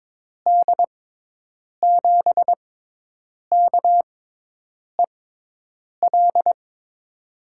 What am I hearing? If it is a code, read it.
Morse code, heard as a keyed tone, about 22 wpm, 700 Hz. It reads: D 7 K E L
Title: USA Call Signs - 6x Character Spacing - 22wpm